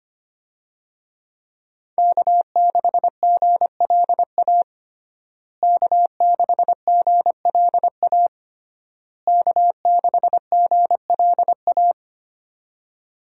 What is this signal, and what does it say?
Morse code, heard as a keyed tone, about 25 wpm, 700 Hz. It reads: K6GLA K6GLA K6GLA
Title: USA Call Signs - 5 Characters - Repeated 3x - 25wpm